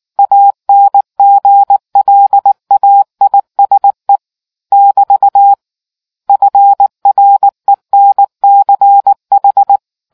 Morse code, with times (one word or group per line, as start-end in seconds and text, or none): 0.2-4.2 ANGLAISE
4.7-5.5 <BT>
6.3-9.8 FRENCH